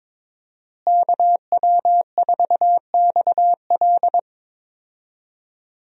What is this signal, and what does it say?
Morse code, heard as a keyed tone, about 22 wpm, 700 Hz. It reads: KW4XL
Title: USA Call Signs - 5 Characters - Repeated 3x - 22wpm